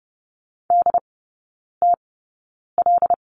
Morse code, heard as a keyed tone, about 30 wpm, spaced as for 9 wpm, 700 Hz. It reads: DTL
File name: 081 Lesson 14e - Number 2 - Three Characters 30wpm